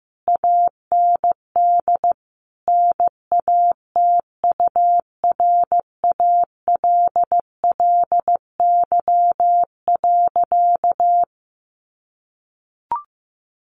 Morse code, heard as a keyed tone, about 15 wpm, 700 Hz. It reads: AND NATURALLY.
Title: Sentences from Top 1300 Words 15wpm